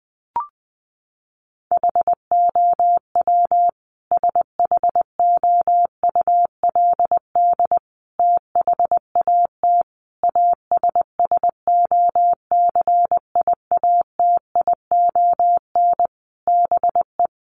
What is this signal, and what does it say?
Morse code, heard as a keyed tone, about 19 wpm, 700 Hz. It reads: HOW SHOULD THAT ASSOCIATION BE
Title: Sentences from Top 1300 Words 20wpm